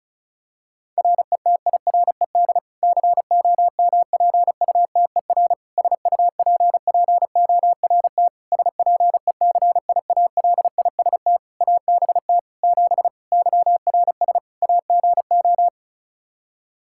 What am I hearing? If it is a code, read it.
Morse code, heard as a keyed tone, about 34 wpm, 700 Hz. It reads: RETIRED COMPUTER SUPPORT SPECIALIST ABT 7 YRS AGO
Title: Practice QSO - Exchange 3 - More Info - 35wpm